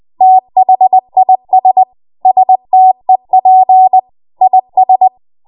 Morse code, heard as a keyed tone, about 20 wpm, 750 Hz. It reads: THIS STEP IS
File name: Morse / 20WPM